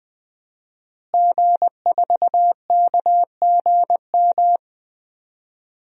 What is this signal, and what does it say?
Morse code, heard as a keyed tone, about 20 wpm, 700 Hz. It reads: G4KGM
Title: DXCC-100 Call Signs - 5 Characters - Repeated 2x - 20wpm